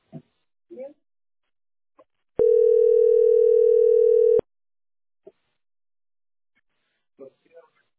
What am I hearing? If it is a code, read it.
Ringback tone (North America), 440 + 480 Hz.